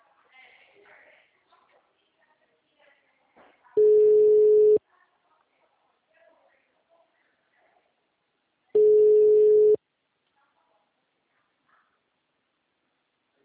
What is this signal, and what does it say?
Ringback tone (Europe, 425 Hz), with speech.